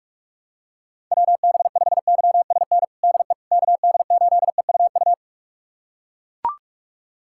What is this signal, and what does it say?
Morse code, heard as a keyed tone, about 41 wpm, 700 Hz. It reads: WB5YSN DE KD8EUU
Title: USA Callsign DE Callsign - 6 Characters - 45wpm